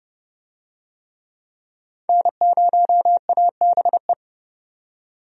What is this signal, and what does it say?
Morse code, heard as a keyed tone, about 30 wpm, 700 Hz. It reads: N0ABE
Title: USA Call Signs - 5 Characters - 30wpm